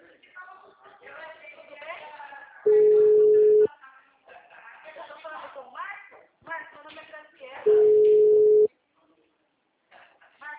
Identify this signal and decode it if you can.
Ringback tone (Europe, 425 Hz), with speech.
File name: out- -2014- - - .97360